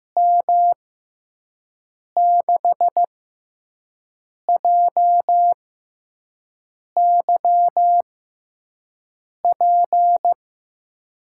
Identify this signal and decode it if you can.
Morse code, heard as a keyed tone, about 15 wpm, spaced as for 5 wpm, 700 Hz. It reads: M6JYP
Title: USA Call Signs - 6 Characters - 6x Character Spacing - 15wpm